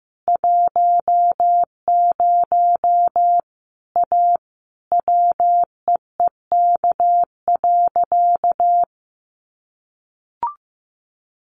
Morse code, heard as a keyed tone, about 15 wpm, 700 Hz. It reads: 10 A WEEK.
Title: Easy Phrases 15wpm